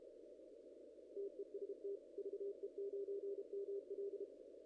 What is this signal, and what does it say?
Morse code, heard as a keyed tone, about 31 wpm, 410 Hz. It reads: TEST VE9ML
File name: contests / ss-cw-k5tr-op-n5rz -1800